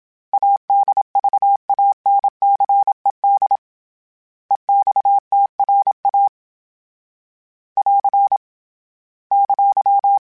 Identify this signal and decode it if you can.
Morse code, heard as a keyed tone, about 27 wpm, 800 Hz. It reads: ADVANCED EXTRA <AR> !
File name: fromSMs / FromSMs mp3s / Classes@25